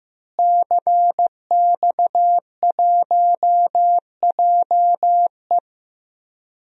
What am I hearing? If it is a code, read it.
Morse code, heard as a keyed tone, about 15 wpm, 700 Hz. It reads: CX1JE
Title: DXCC-100 Call Signs - 5 Characters - Repeated 3x - 15wpm